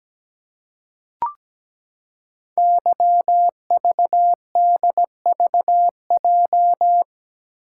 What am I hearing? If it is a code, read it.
Morse code, heard as a keyed tone, about 17 wpm, 700 Hz. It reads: YVDVJ
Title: Code Groups - 5 Letters - 17wpm